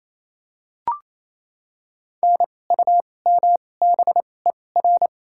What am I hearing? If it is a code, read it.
Morse code, heard as a keyed tone, about 28 wpm, spaced as for 20 wpm, 700 Hz. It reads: NUMBER
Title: 2x Character Spacing - Top 500 Words - 28wpm